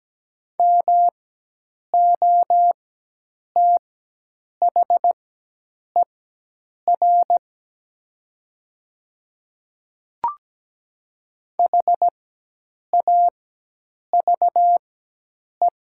Morse code, heard as a keyed tone, about 17 wpm, spaced as for 8 wpm, 700 Hz. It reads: MOTHER HAVE
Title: 4x Character Spacing - Top 300 Words - 17wpm